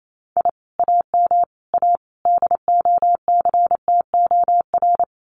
Morse code, heard as a keyed tone, about 28 wpm, 700 Hz. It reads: I AM A DOCTOR
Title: Common QSO Phrases Encore 28wpm